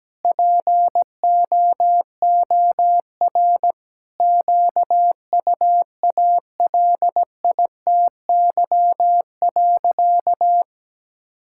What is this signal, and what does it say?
Morse code, heard as a keyed tone, about 17 wpm, 700 Hz. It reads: POOR QUALITY.
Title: Sentences from Top 1000 Words - Review Part 5 - 17wpm